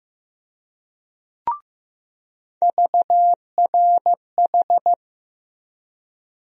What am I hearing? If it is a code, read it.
Morse code, heard as a keyed tone, about 15 wpm, 700 Hz. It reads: VRH
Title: USA Call Sign Postfixes - 3 Letters 15wpm